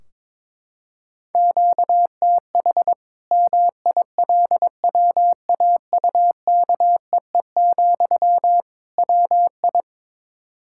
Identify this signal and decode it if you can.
Morse code, heard as a keyed tone, about 22 wpm, 700 Hz. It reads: QTH MILWAUKEE, WI
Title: Practice QSO - Exchange 1 - Beginning - 22wpm